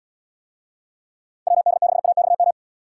Morse code, heard as a keyed tone, about 78 wpm, 700 Hz. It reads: WD6AXM